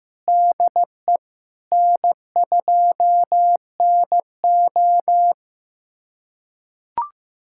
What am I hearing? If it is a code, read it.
Morse code, heard as a keyed tone, about 15 wpm, 700 Hz. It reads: DE N2NO
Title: USA Callsign DE Callsign 15wpm